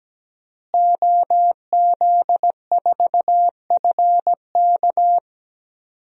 Morse code, heard as a keyed tone, about 17 wpm, 700 Hz. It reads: OZ4FK
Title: DXCC-100 Call Signs - 5 Characters - Repeated 2x - 17wpm